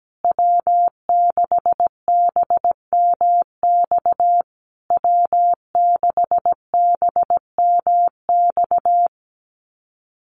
Morse code, heard as a keyed tone, about 17 wpm, 700 Hz. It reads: W6BMX W6BMX
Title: USA Call Signs Repeated 3x 17wpm